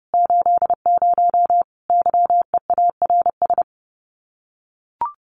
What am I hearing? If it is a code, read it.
Morse code, heard as a keyed tone, about 30 wpm, 700 Hz. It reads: 80 YEARS
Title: Common QSO Phrases 30wpm